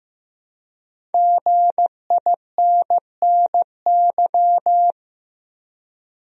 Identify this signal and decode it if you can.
Morse code, heard as a keyed tone, about 15 wpm, 700 Hz. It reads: GINNY
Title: USA Common Names - 15wpm